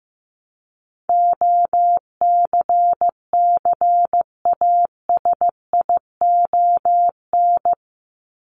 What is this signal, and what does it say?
Morse code, heard as a keyed tone, about 15 wpm, 700 Hz. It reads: OCCASION
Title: Top 1501-1750 Words 15wpm